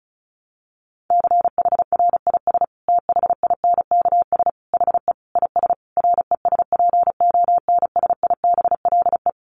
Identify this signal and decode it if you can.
Morse code, heard as a keyed tone, about 34 wpm, 700 Hz. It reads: CHRIS THINKS HE IS RESPONSIBLE
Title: Additional Easy Phrases 35wpm